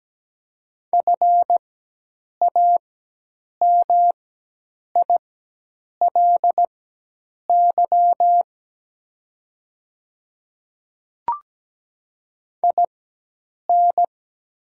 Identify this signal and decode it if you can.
Morse code, heard as a keyed tone, about 17 wpm, spaced as for 8 wpm, 700 Hz. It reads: FAMILY IN
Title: 4x Character Spacing - Top 500 Words - 17wpm